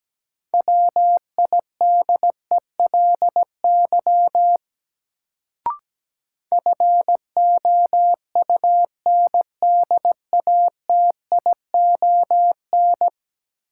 Morse code, heard as a keyed tone, about 17 wpm, 700 Hz. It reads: WIDELY FOUNDATION
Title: Sentences from Top 1200 Words - New Word Drill Alt - 17wpm